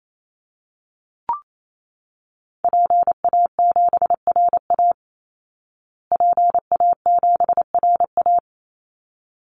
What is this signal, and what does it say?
Morse code, heard as a keyed tone, about 28 wpm, 700 Hz. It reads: PA7RA PA7RA
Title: Call Signs Repeated 3x - 28wpm